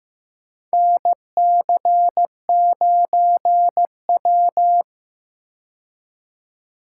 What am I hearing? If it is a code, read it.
Morse code, heard as a keyed tone, about 15 wpm, 700 Hz. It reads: NC9W
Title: K1USN SST QSO Practice - 15wpm